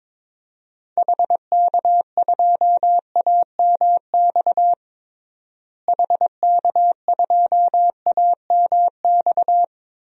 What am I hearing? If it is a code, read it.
Morse code, heard as a keyed tone, about 22 wpm, 700 Hz. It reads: HK2AMX HK2AMX
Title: DXCC-100 Call Signs - 6 Characters - Repeated 2x - 22wpm